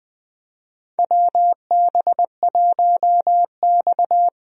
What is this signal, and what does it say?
Morse code, heard as a keyed tone, about 20 wpm, 700 Hz. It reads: WB1X
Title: USA Call Signs - 4 Characters - Repeated 2x - 20wpm